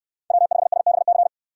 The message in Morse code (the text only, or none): W5SBC